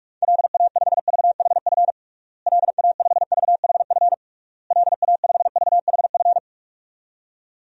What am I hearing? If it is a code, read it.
Morse code, heard as a keyed tone, about 46 wpm, 700 Hz. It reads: LA5VHF LA5VHF LA5VHF
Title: DXCC-100 Call Signs - 6 Characters - Repeated 3x - 45wpm